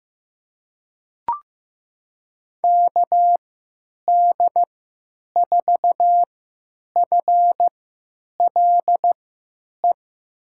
Morse code, heard as a keyed tone, about 16 wpm, spaced as for 9 wpm, 700 Hz. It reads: KD4FLE
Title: USA Call Signs - 6 Characters - 3x Character Spacing - 15wpm